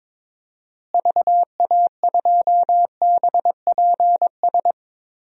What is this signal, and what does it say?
Morse code, heard as a keyed tone, about 22 wpm, 700 Hz. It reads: VA2BPS
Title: DXCC-100 Call Signs - 6 Characters - Repeated 2x - 22wpm